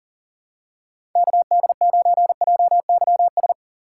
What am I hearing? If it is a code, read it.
Morse code, heard as a keyed tone, about 40 wpm, 700 Hz. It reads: KD9JYS
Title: USA Call Signs - 6 Characters - 6x Character Spacing - 40wpm